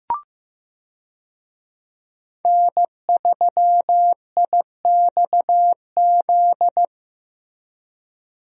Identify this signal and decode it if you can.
Morse code, heard as a keyed tone, about 15 wpm, 700 Hz. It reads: N3IXZ
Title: USA Call Signs - 5 Characters - Repeated 2x - 15wpm